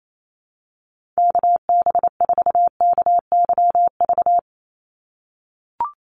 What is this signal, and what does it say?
Morse code, heard as a keyed tone, about 28 wpm, 700 Hz. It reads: KB4KYV
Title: USA Call Signs 28wpm